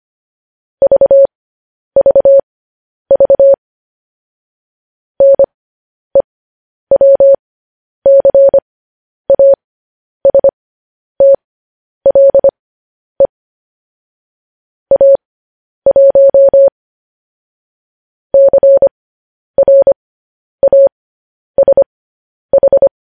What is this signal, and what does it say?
Morse code, heard as a keyed tone, about 25 wpm, spaced as for 10 wpm, 550 Hz. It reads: VVV NEWCASTLE A1 CRASH
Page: news headlines in morse code at 10 wpm